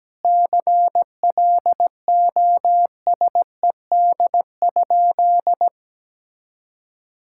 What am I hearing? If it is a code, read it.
Morse code, heard as a keyed tone, about 17 wpm, 700 Hz. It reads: CLOSED?
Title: Sentences from Top 1600 Words 17wpm